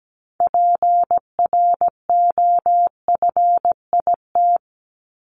Morse code, heard as a keyed tone, about 17 wpm, 700 Hz. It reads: PROFIT
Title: Top 751-1000 Words 17wpm